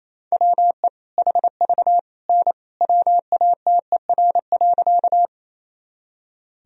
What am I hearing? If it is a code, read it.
Morse code, heard as a keyed tone, about 28 wpm, 700 Hz. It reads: WE HV N WATER.
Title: Sentences from Top 100 Words and Abbreviations 28wpm